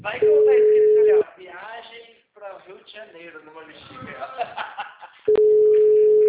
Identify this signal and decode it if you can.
Ringback tone (Europe, 425 Hz), with speech.
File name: out- -2003- - - .56609